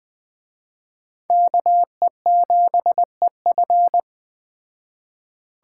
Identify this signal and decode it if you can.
Morse code, heard as a keyed tone, about 21 wpm, 700 Hz. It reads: KE7EF